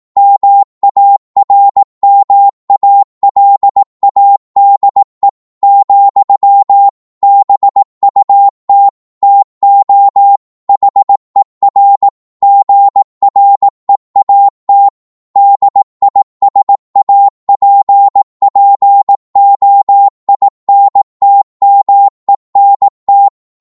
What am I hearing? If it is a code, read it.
Morse code, heard as a keyed tone, about 18 wpm, 800 Hz. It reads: MARMALADE, BUT TO HER GREAT DISAPPOINTMENT